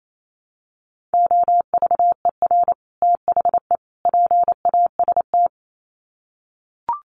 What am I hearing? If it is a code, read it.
Morse code, heard as a keyed tone, about 28 wpm, 700 Hz. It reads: OVER THE PAST
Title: Sets of 3 Words 28wpm